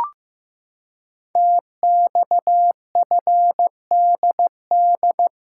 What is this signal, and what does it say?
Morse code, heard as a keyed tone, about 15 wpm, 700 Hz. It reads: TXFDD
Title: Code Groups - 5 Letters - 15wpm